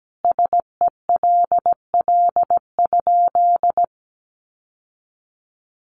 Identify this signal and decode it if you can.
Morse code, heard as a keyed tone, about 17 wpm, 700 Hz. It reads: SELL?
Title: Sentences from Top 1200 Words 17wpm